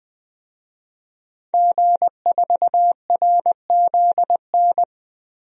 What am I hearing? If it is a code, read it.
Morse code, heard as a keyed tone, about 21 wpm, 700 Hz. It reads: G4RZN
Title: DXCC-100 Call Signs - 5 Characters - 20wpm